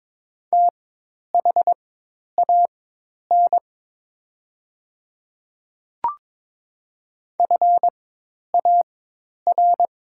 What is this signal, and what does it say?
Morse code, heard as a keyed tone, about 22 wpm, spaced as for 10 wpm, 690 Hz. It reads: THAN FAR
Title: 4x Character Spacing - Top 300 Words - 22wpm